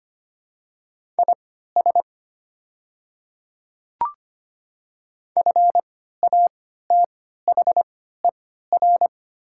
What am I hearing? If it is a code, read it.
Morse code, heard as a keyed tone, about 25 wpm, spaced as for 14 wpm, 700 Hz. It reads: IS FATHER
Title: 3x Character Spacing - Top 200 Words - 25wpm